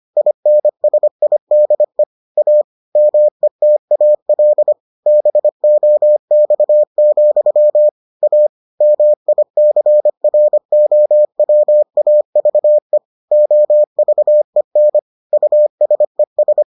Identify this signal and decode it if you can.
Morse code, heard as a keyed tone, about 25 wpm, 600 Hz. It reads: INSIDE A METAL BOX, A MICROWAVE OVEN USES